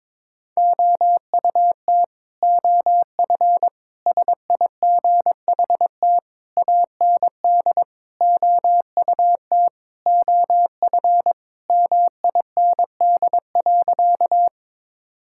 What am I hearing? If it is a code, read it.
Morse code, heard as a keyed tone, about 22 wpm, 700 Hz. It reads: OUT OF SIGHT AND OUT OF MIND.
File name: Sentences with 7-Word Phrases - 22wpm